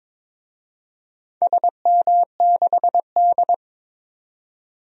Morse code, heard as a keyed tone, about 22 wpm, 700 Hz. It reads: SM6D
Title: DXCC-100 Call Signs - 4 Characters - 22wpm